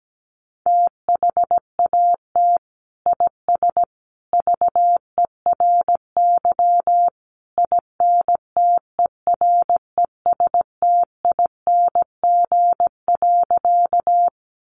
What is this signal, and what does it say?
Morse code, heard as a keyed tone, about 17 wpm, 700 Hz. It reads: THAT IS VERY INTERESTING.
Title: Sentences from Top 1000 Words 17wpm